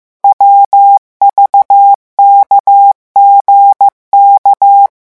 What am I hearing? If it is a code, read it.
Morse code, heard as a keyed tone, about 15 wpm, 800 Hz. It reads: WVKGK
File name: morse mp3 / revision11 lettres vitesse 14